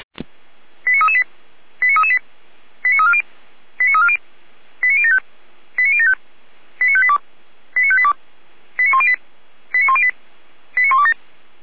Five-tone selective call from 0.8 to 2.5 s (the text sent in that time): 89338 89338